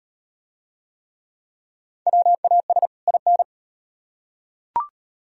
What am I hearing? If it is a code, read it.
Morse code, heard as a keyed tone, about 38 wpm, 700 Hz. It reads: WAS IN